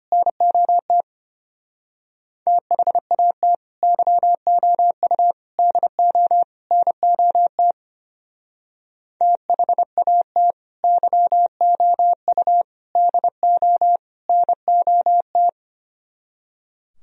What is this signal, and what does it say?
Morse code, heard as a keyed tone, about 32 wpm, 700 Hz. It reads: NOT THAT YOU DO NOT THAT YOU DO NOT